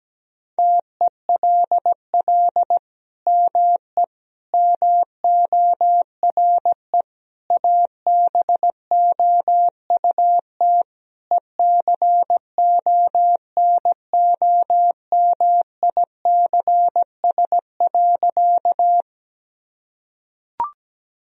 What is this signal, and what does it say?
Morse code, heard as a keyed tone, about 17 wpm, 700 Hz. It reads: TELL ME MORE ABOUT ECONOMICS.
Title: Sentences from Top 1000 Words - Review Part 3 - 17wpm